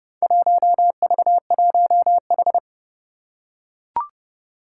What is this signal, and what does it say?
Morse code, heard as a keyed tone, about 30 wpm, 700 Hz. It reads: 1V1H
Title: Four Challenging Letter-Number Combinations 30wpm